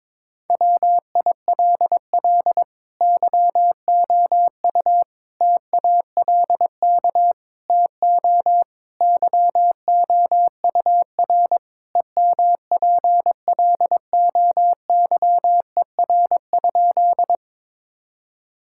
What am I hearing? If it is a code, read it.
Morse code, heard as a keyed tone, about 22 wpm, 700 Hz. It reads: WILL YOU TALK TO YOUR EMPLOYER?
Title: Sentences from Top 1600 Words 22wpm